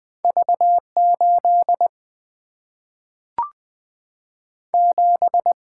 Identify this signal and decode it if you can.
Morse code, heard as a keyed tone, about 20 wpm, 700 Hz. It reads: V 8 7
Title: Single-Letter Single-Number 20wpm